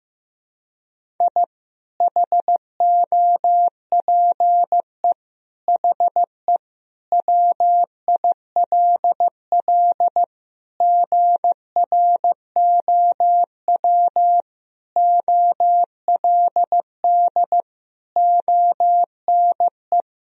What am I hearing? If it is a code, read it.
Morse code, heard as a keyed tone, about 15 wpm, 700 Hz. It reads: I HOPE HE WILL GROW OLD ONE
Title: Sentences from Top 1000 Words - Review Part 1 - 15wpm